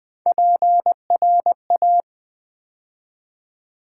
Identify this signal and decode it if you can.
Morse code, heard as a keyed tone, about 21 wpm, 700 Hz. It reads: PRA
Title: Build A Word - 3 Letters At A Time - 8-Letter Words - 20wpm